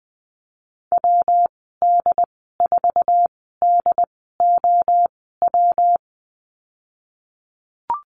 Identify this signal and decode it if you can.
Morse code, heard as a keyed tone, about 20 wpm, spaced as for 15 wpm, 700 Hz. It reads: WD4DOW
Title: 167 Lesson 27g - Number 7 - Call Signs 20wpm